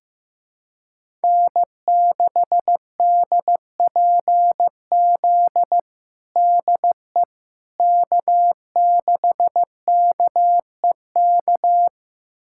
Morse code, heard as a keyed tone, about 16 wpm, 700 Hz. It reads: N6DPZ DE K6KEK
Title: USA Callsign DE Callsign - 5 Characters - 15wpm